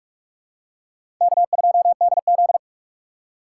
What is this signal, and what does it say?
Morse code, heard as a keyed tone, about 46 wpm, 700 Hz. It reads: K2DZ
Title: USA Call Signs - 4x Character Spacing - 45wpm